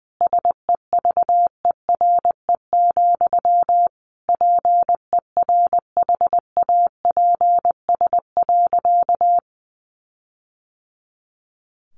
Morse code, heard as a keyed tone, about 20 wpm, 700 Hz. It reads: SEVERE, PERHAPS.
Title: Sentences from Top 1400 Words 20wpm